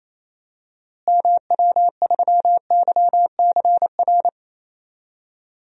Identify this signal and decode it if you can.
Morse code, heard as a keyed tone, about 28 wpm, 700 Hz. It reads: MW3YCR
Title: DXCC-100 Call Signs - 6 Characters - Repeated 3x - 28wpm